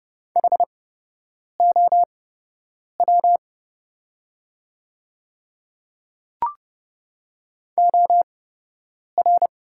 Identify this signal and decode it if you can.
Morse code, heard as a keyed tone, about 29 wpm, spaced as for 8 wpm, 700 Hz. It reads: HOW OR
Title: 8x Character Spacing - Top 100 Words - 30wpm